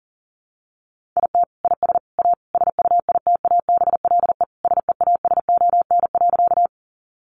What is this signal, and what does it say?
Morse code, heard as a keyed tone, about 40 wpm, 700 Hz. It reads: IT IS A SUITABLE SEASON.
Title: Sentences from Top 1400 Words 40wpm